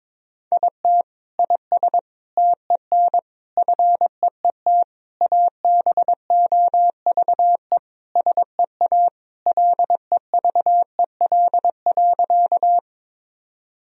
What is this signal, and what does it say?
Morse code, heard as a keyed tone, about 21 wpm, 700 Hz. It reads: IT IS TEN FEET ABOVE SEA LEVEL.
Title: Sentences from Top 1000 Words - Review Part 4 - 22wpm